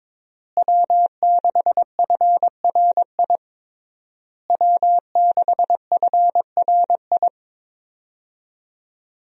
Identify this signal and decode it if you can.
Morse code, heard as a keyed tone, about 22 wpm, 700 Hz. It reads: W6FRI W6FRI
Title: USA Call Signs - 5 Characters - Repeated 3x - 22wpm